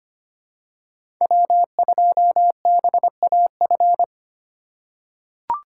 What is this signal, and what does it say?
Morse code, heard as a keyed tone, about 25 wpm, 700 Hz. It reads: W2BAF
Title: USA Call Signs - 5 Characters - 4x Character Spacing - 25wpm